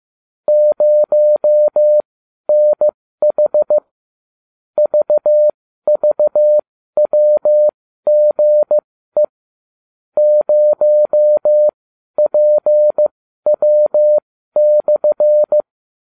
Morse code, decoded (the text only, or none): 0NH VVWGE 0PW/